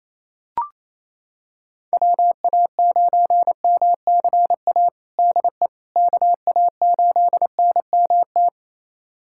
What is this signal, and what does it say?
Morse code, heard as a keyed tone, about 28 wpm, 700 Hz. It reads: WA9MCA DE KA8NMT
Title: USA Callsign DE Callsign - 6 Characters - 28wpm